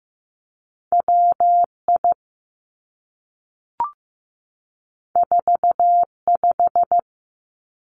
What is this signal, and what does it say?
Morse code, heard as a keyed tone, about 15 wpm, 700 Hz. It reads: WI 45
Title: Two Challenging Letter-Number Combinations 15wpm